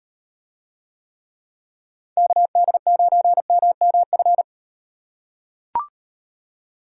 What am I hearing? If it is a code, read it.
Morse code, heard as a keyed tone, about 37 wpm, 700 Hz. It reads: KD9MMF